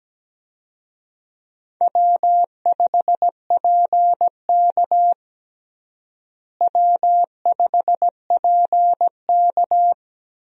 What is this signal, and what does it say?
Morse code, heard as a keyed tone, about 17 wpm, 700 Hz. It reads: W5PK W5PK
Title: USA Call Signs - 4 Characters - Repeated 2x - 17wpm